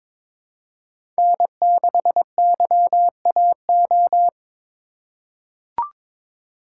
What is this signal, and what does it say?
Morse code, heard as a keyed tone, about 22 wpm, 700 Hz. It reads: N6YAO
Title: USA Call Signs - 5 Characters - 4x Character Spacing - 22wpm